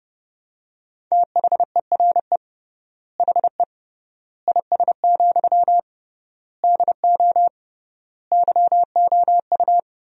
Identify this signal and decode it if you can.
Morse code, heard as a keyed tone, about 32 wpm, 700 Hz. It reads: THERE HE IS, DO YOU
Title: 3x Word-Spacing - Sentences from Top 100 Words - 30wpm